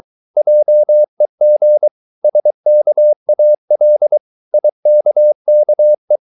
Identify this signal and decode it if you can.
Morse code, heard as a keyed tone, about 23 wpm, 600 Hz. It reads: JEG SKAL IKKE
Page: Ferie 23 wpm